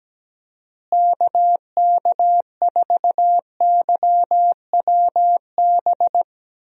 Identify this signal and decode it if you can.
Morse code, heard as a keyed tone, about 17 wpm, 700 Hz. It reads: KK4YWB
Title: USA Call Signs - 6 Characters - Repeated 3x - 17wpm